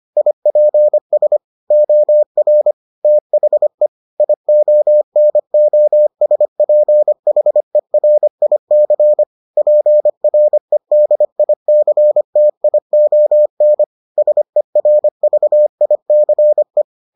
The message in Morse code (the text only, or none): IPS OR THE IONOSPHERIC PREDICTION SERVICE